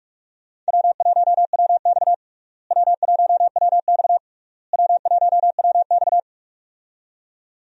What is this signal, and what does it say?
Morse code, heard as a keyed tone, about 46 wpm, 700 Hz. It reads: W1WX W1WX W1WX
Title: USA Call Signs - 4 Characters - Repeated 3x - 45wpm